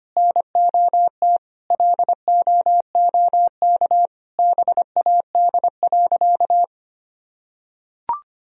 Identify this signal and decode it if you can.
Morse code, heard as a keyed tone, about 25 wpm, 700 Hz. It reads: NOT LOOK BAD.
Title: Sentences from Top 1000 Words - Review Part 3 - 25wpm